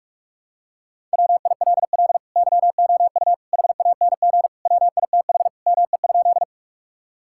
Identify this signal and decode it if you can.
Morse code, heard as a keyed tone, about 43 wpm, 700 Hz. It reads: WILL YOU HANG WITH ME?
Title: Sentences from Top 1600 Words 45wpm